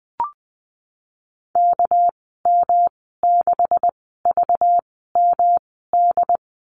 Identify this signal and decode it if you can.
Morse code, heard as a keyed tone, about 19 wpm, spaced as for 14 wpm, 700 Hz. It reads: KM6VMD
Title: 200 Lesson 32g - Letter k - Call Signs 20wpm